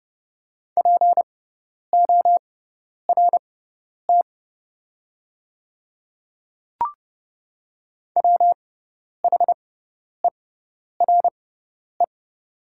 Morse code, heard as a keyed tone, about 30 wpm, spaced as for 10 wpm, 700 Hz. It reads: PORT WHERE
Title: 6x Character Spacing - Top 200 Words - 30wpm